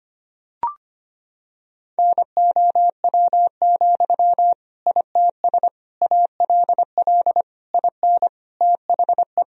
Morse code, heard as a keyed tone, about 25 wpm, 700 Hz. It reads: NOW, ITS ALL IN THE
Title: Sentences from Top 1000 Words - Review Part 3 - 25wpm